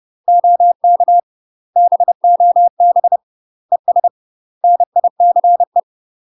Morse code, heard as a keyed tone, about 30 wpm, 700 Hz. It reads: OK BOB ES NICE
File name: UK MASTER QSOs TRAINING / 30WPM / UK MASTER QSOs 0029